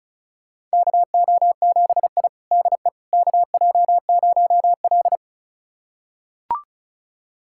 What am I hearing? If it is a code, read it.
Morse code, heard as a keyed tone, about 36 wpm, 700 Hz. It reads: KO7I DE KJ0L
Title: USA Callsign DE Callsign - 4 Characters - 35wpm